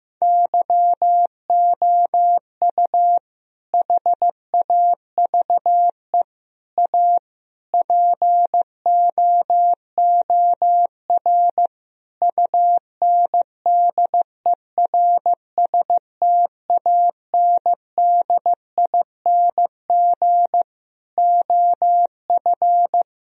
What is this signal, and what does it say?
Morse code, heard as a keyed tone, about 15 wpm, 700 Hz. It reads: YOU HAVE A POOR UNDERSTANDING OF
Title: Sentences from Top 1000 Words - Review Part 3 - 15wpm